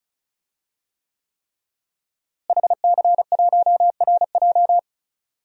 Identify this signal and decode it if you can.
Morse code, heard as a keyed tone, about 36 wpm, 700 Hz. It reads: HC1RJ
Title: DXCC-100 Call Signs - 5 Characters - Repeated 3x - 35wpm